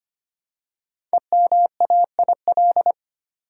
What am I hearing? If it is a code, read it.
Morse code, heard as a keyed tone, about 25 wpm, 700 Hz. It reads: EMAIL
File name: 5 Letter Words - First 3 Letters Given - 25wpm